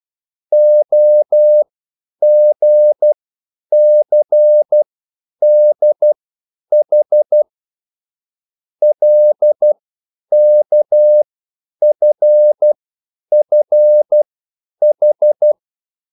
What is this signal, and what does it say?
Morse code, heard as a keyed tone, about 12 wpm, spaced as for 9 wpm, 600 Hz. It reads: OGCDH LKFFH
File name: QSL / CW lesson